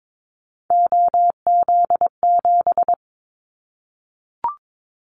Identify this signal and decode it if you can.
Morse code, heard as a keyed tone, about 22 wpm, 700 Hz. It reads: OZ7
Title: Callsign Prefixes for Top 100 DXCC - 22wpm